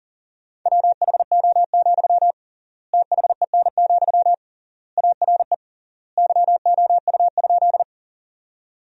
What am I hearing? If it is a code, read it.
Morse code, heard as a keyed tone, about 40 wpm, 700 Hz. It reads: WHO, THEN, ARE YOU?
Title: 3x Word-Spacing - Sentences from Top 100 Words - 40wpm